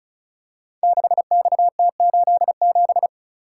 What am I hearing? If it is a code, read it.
Morse code, heard as a keyed tone, about 36 wpm, 690 Hz. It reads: 6XT87
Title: Code Groups - 5 Letter:Numbers - 35wpm